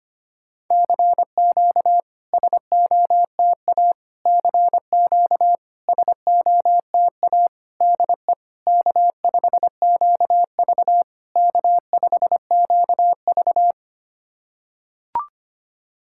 Morse code, heard as a keyed tone, about 25 wpm, 700 Hz. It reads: CQ SOTA CQ SOTA DE K5QV K5QV
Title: SOTA QSO Practice 25wpm